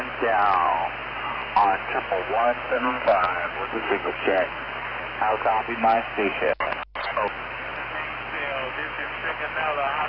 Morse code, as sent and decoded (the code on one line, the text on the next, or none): none